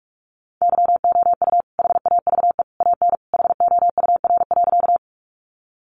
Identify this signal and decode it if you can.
Morse code, heard as a keyed tone, about 44 wpm, 700 Hz. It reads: YOU HAVE AN HOUR.
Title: Sentences from Top 400 Words 45wpm